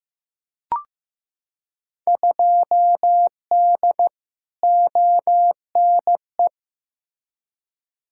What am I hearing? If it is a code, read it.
Morse code, heard as a keyed tone, about 15 wpm, 700 Hz. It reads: 2D ONE
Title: Field Day Exchanges - 15wpm